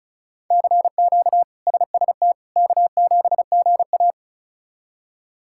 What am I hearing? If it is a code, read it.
Morse code, heard as a keyed tone, about 34 wpm, 700 Hz. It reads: CQ SST K7GA
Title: K1USN SST QSO Practice - 35wpm